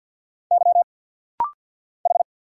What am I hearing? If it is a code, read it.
Morse code, heard as a keyed tone, about 50 wpm, 690 Hz. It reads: Y H
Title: Single Letter-Number Alt - Rapid-Fire 50wpm